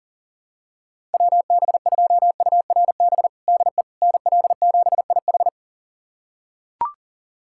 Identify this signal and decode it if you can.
Morse code, heard as a keyed tone, about 37 wpm, 700 Hz. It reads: WB2URB DE NL7IH
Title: USA Callsign DE Callsign 40wpm